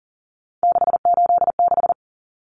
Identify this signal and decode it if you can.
Morse code, heard as a keyed tone, about 40 wpm, 700 Hz. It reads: 686
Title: Common Cut Numbers - 3-Digit 40wpm